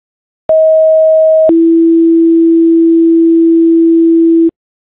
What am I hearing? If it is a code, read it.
Two-tone page, heard as a one-second tone, then a three-second tone, about 640 then 330 Hz. It is heard from 0.5 to 4.5 s.